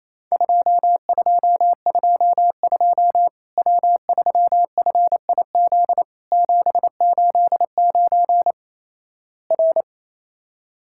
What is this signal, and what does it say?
Morse code, heard as a keyed tone, about 28 wpm, 700 Hz. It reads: 2222 W3FIZ 789 R
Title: Worked All Europe DX Contest - QTC Exchange - 28wpm